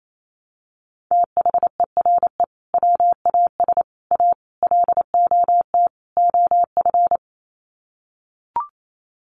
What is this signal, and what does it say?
Morse code, heard as a keyed tone, about 28 wpm, 700 Hz. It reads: THERE WAS A LOT OF
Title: Sets of 5 Words 28wpm